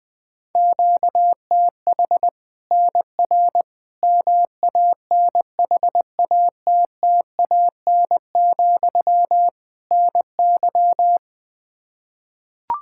Morse code, heard as a keyed tone, about 20 wpm, 700 Hz. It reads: QTH NR MANHATTAN, NY
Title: Practice QSO - Exchange 1 - Beginning - 20wpm